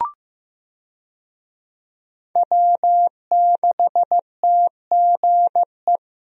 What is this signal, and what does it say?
Morse code, heard as a keyed tone, about 15 wpm, 700 Hz. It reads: W6TGE
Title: USA Call Signs - 5 Characters - Repeated 3x - 15wpm